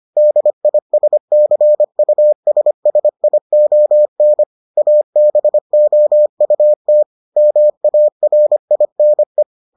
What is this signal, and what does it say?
Morse code, heard as a keyed tone, about 25 wpm, 600 Hz. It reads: DISCUSSION ABOUT MARINE